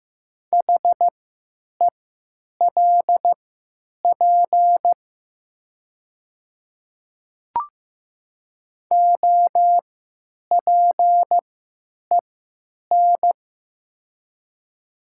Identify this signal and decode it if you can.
Morse code, heard as a keyed tone, about 15 wpm, spaced as for 9 wpm, 700 Hz. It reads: HELP OPEN
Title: 3x Character Spacing - Top 300 Words - 15wpm